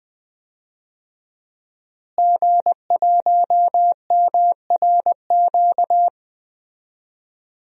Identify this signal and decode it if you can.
Morse code, heard as a keyed tone, about 20 wpm, 700 Hz. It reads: G1MRQ
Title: DXCC-100 Call Signs - 5 Characters - Repeated 2x - 20wpm